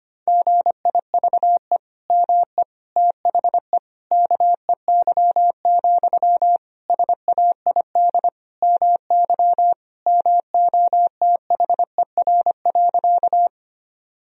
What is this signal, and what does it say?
Morse code, heard as a keyed tone, about 25 wpm, 700 Hz. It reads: GIVE ME THE KEY, SAID MY MOTHER.
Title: Sentences from Top 1000 Words - Review Part 4 - 25wpm